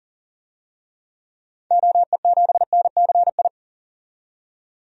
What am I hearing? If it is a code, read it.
Morse code, heard as a keyed tone, about 40 wpm, 700 Hz. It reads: OE7NCI
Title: DXCC-100 Call Signs - 6 Characters - 40wpm